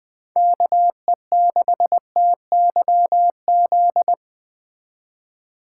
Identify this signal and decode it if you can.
Morse code, heard as a keyed tone, about 20 wpm, 700 Hz. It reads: KE6TYZ
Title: Speed Racing - USA Call Signs 17wpm